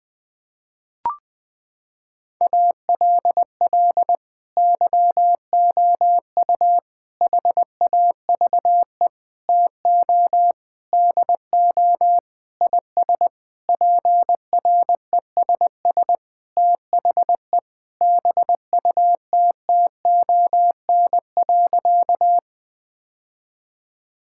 Morse code, heard as a keyed tone, about 19 wpm, 700 Hz. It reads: ALL YOU HAVE TO DO IS PRESS THE BUTTON.
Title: Sentences with 6-Word Phrases - 20wpm